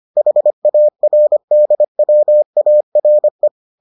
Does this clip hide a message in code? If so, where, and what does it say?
Yes: Morse code from 0.2 to 3.5 s, heard as a keyed tone, about 25 wpm, 600 Hz. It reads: HARDWARE